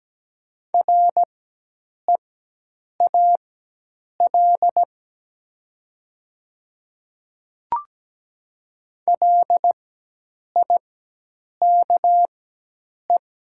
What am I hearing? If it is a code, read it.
Morse code, heard as a keyed tone, about 17 wpm, spaced as for 8 wpm, 700 Hz. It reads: REAL LIKE
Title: 4x Character Spacing - Top 300 Words - 17wpm